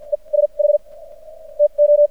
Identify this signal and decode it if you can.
Morse code, heard as a keyed tone, about 56 wpm, spaced as for 37 wpm, 590 Hz. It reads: ESH TO
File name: archived / CRSnet